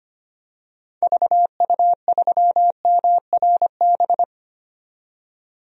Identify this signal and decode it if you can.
Morse code, heard as a keyed tone, about 25 wpm, 700 Hz. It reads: VU3MRB
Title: DXCC-100 Call Signs - 6 Characters - 25wpm